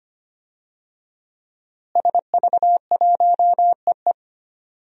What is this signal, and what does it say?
Morse code, heard as a keyed tone, about 25 wpm, 700 Hz. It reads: SV1EE
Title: DXCC-100 Call Signs - 5 Characters - Repeated 2x - 25wpm